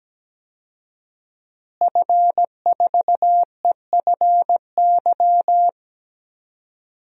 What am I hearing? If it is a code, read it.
Morse code, heard as a keyed tone, about 17 wpm, 700 Hz. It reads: F4EFY
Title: DXCC-100 Call Signs - 5 Characters - Repeated 3x - 17wpm